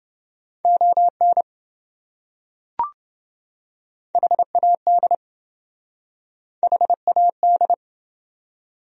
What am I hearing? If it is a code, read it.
Morse code, heard as a keyed tone, about 27 wpm, 700 Hz. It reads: ON HAD HAD